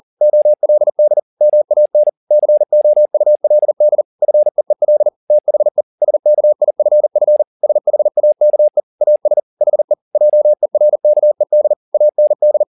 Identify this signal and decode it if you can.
Morse code, heard as a keyed tone, about 40 wpm, 600 Hz. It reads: OLD MAN COULD FEEL THE SKIFF SHAKE AS HE JERKED AND
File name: the old man and the sea 2 40wpm